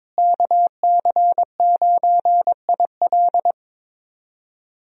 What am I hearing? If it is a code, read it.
Morse code, heard as a keyed tone, about 22 wpm, 700 Hz. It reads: KC9IL
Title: USA Call Signs - 5 Characters - Repeated 2x - 22wpm